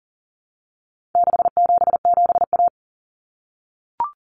Morse code, heard as a keyed tone, about 40 wpm, 700 Hz. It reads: 677A
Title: Common Cut Numbers - 4-Digit 40wpm